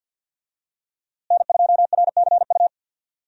Morse code, heard as a keyed tone, about 52 wpm, 700 Hz. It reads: N2RCU
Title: USA Call Signs - 5 Characters - Repeated 2x - 50wpm